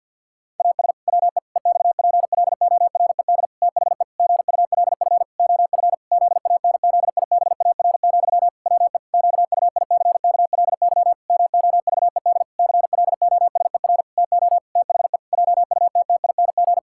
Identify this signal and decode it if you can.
Morse code, heard as a keyed tone, about 50 wpm, 700 Hz. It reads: AS WE EXPLORED THE GULF OF ZANZIBAR, WE QUICKLY MOVED CLOSER TO THE JUTTING
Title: Pangram Challenge v2 - 50wpm